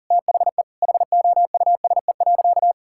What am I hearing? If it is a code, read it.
Morse code, heard as a keyed tone, about 40 wpm, 700 Hz. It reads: THE HOUSE.
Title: Sentences from Top 1600 Words 40wpm